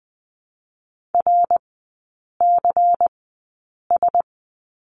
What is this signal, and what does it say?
Morse code, heard as a keyed tone, about 20 wpm, spaced as for 8 wpm, 700 Hz. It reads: RCS
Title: 114 Lesson 19e - Letter w - Three Characters 20wpm